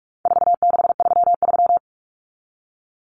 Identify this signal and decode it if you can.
Morse code, heard as a keyed tone, about 44 wpm, 700 Hz. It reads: 4633
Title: Common Cut Numbers - 4-Digit 45wpm